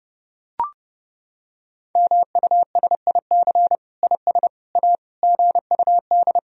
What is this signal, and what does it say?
Morse code, heard as a keyed tone, about 30 wpm, 700 Hz. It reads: MUSIC IS A GUD
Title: Sentences from Top 300 Words and Abbreviations 30wpm